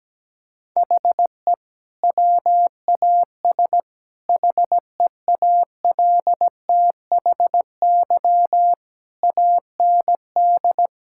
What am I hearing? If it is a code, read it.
Morse code, heard as a keyed tone, about 17 wpm, 700 Hz. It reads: HE WAS HEALTHY AND
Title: Sentences from Top 1000 Words - Review Part 6 - 17wpm